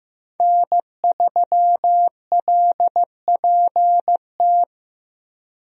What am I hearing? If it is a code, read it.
Morse code, heard as a keyed tone, about 15 wpm, 700 Hz. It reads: N3LPT
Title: USA Call Signs - 5 Characters - Repeated 2x - 15wpm